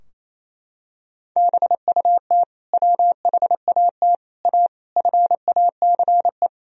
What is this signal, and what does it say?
Morse code, heard as a keyed tone, about 28 wpm, 700 Hz. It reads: BUT WHAT A FACE
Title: Sentences from Top 1000 Words - Review Part 2 - 28wpm